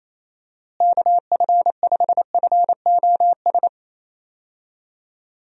Morse code, heard as a keyed tone, about 28 wpm, 700 Hz. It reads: KF5FOS
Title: USA Call Signs - 6 Characters - Repeated 3x - 28wpm